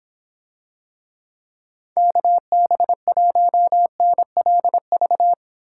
Morse code, heard as a keyed tone, about 26 wpm, 700 Hz. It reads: KB1NLV